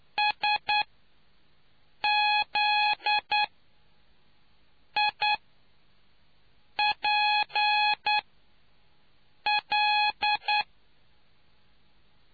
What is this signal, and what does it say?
Morse code, heard as a keyed tone, about 9 wpm, spaced as for 5 wpm, 840 Hz. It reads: SZIPL